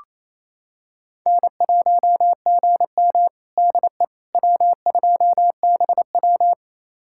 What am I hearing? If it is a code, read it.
Morse code, heard as a keyed tone, about 28 wpm, 700 Hz. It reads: N1GM DE W2BW
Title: USA Callsign DE Callsign - 4 Characters - 28wpm